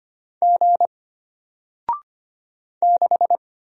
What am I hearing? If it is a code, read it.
Morse code, heard as a keyed tone, about 25 wpm, 700 Hz. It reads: G 6
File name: Single-Letter-Number-25wpm